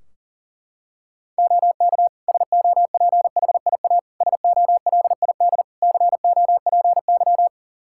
Morse code, heard as a keyed tone, about 40 wpm, 700 Hz. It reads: OK SOPHIA SOLID COPY
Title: Practice QSO - Exchange 3 - More Info - 40wpm